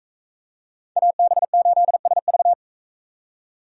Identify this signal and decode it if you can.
Morse code, heard as a keyed tone, about 44 wpm, 700 Hz. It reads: AB8SV